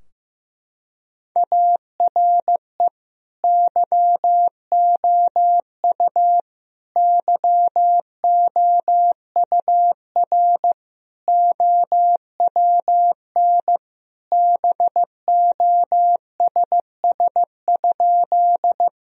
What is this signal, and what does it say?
Morse code, heard as a keyed tone, about 15 wpm, 700 Hz. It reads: ARE YOU YOUR OWN BOSS?
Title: Sentences from Top 1000 Words - Review Part 4 - 15wpm